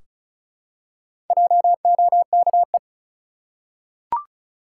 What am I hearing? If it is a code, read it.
Morse code, heard as a keyed tone, about 36 wpm, 700 Hz. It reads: JOKE
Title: Four Letter Words - 35wpm